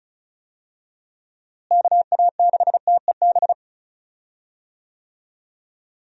Morse code, heard as a keyed tone, about 36 wpm, 700 Hz. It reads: KA6TEB
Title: USA Call Signs - 6 Characters - Repeated 3x - 35wpm